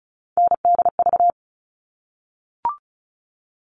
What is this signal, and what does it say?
Morse code, heard as a keyed tone, about 34 wpm, 700 Hz. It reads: NDV
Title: All Cut Numbers - 3-Digit 35wpm